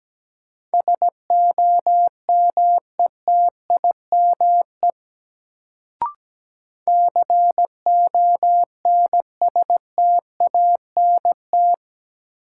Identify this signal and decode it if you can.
Morse code, heard as a keyed tone, about 17 wpm, 700 Hz. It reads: SOMETIME CONSTANT
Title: Sentences from Top 1500 Words - New Word Drill Alt - 17wpm